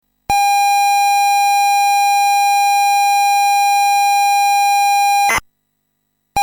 ZX Spectrum tape loading.